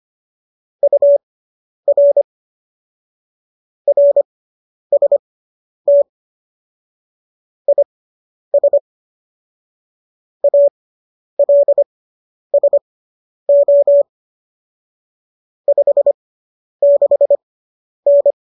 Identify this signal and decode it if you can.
Morse code, heard as a keyed tone, about 25 wpm, spaced as for 10 wpm, 570 Hz. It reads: UR RST IS ALSO 56N